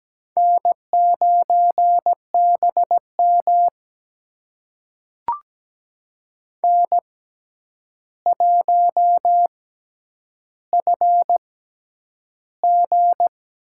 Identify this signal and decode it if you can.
Morse code, heard as a keyed tone, about 17 wpm, spaced as for 9 wpm, 700 Hz. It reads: N9BM N1FG
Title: USA Call Signs - 4 Characters - 6x Character Spacing - 17wpm